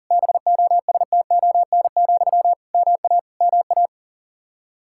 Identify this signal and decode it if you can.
Morse code, heard as a keyed tone, about 40 wpm, 700 Hz. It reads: BOSTON, MA MA
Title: Practice QSO - Exchange 1 - Beginning - 40wpm